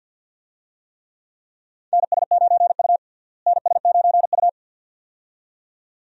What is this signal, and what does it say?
Morse code, heard as a keyed tone, about 48 wpm, 700 Hz. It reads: NS9U NS9U
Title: USA Call Signs - 4 Characters - Repeated 2x - 50wpm